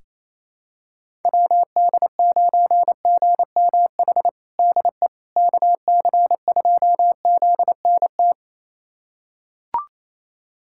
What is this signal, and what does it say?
Morse code, heard as a keyed tone, about 28 wpm, 700 Hz. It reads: WD9GMH DE KC2ZNT
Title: USA Callsign DE Callsign 28wpm